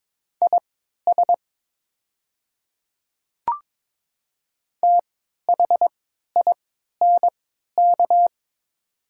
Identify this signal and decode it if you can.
Morse code, heard as a keyed tone, about 22 wpm, spaced as for 12 wpm, 700 Hz. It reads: IS THINK
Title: 3x Character Spacing - Top 200 Words - 22wpm